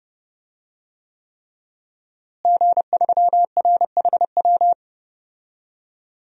Morse code, heard as a keyed tone, about 32 wpm, 700 Hz. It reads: G3RHW
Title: DXCC-100 Call Signs - 5 Characters - Repeated 3x - 30wpm